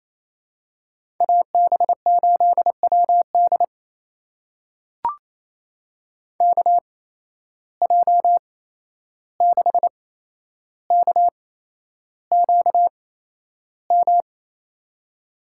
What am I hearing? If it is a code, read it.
Morse code, heard as a keyed tone, about 28 wpm, spaced as for 11 wpm, 700 Hz. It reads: AB8WD KJ6KQM
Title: USA Call Signs - 8x Character Spacing - 28wpm